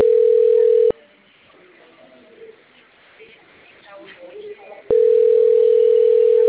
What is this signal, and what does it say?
Ringback tone (North America), 440 + 480 Hz, with speech.
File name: out- - - - - .751